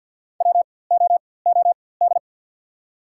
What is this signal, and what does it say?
Morse code, heard as a keyed tone, about 53 wpm, spaced as for 24 wpm, 690 Hz. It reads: WOOD
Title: 4x Character Spacing - Top 300 Words - 50wpm